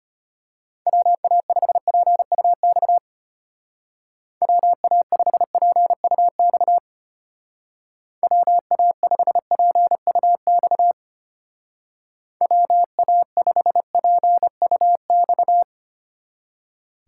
Morse code, heard as a keyed tone, about 32 wpm, 700 Hz. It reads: WA5PUX WA5PUX WA5PUX WA5PUX